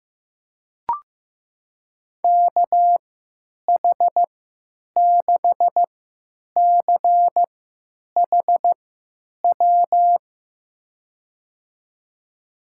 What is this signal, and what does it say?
Morse code, heard as a keyed tone, about 15 wpm, spaced as for 9 wpm, 700 Hz. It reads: KH6CHW
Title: USA Call Signs - 6 Characters - 3x Character Spacing - 15wpm